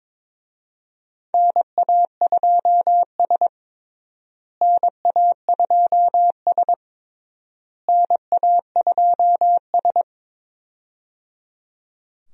Morse code, heard as a keyed tone, about 22 wpm, 700 Hz. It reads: NA2S NA2S NA2S
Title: USA Call Signs - 4 Characters - Repeated 3x - 22wpm